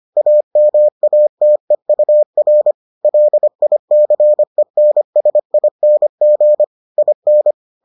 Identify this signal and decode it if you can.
Morse code, heard as a keyed tone, about 25 wpm, 600 Hz. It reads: AMATEUR LICENSING IN